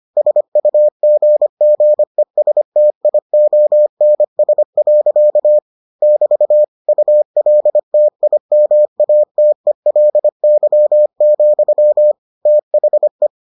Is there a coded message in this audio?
Yes: Morse code from 0.2 to 13.3 s, heard as a keyed tone, about 25 wpm, 600 Hz. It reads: SUGGESTIONS. <BT> ULTIMATELY, THE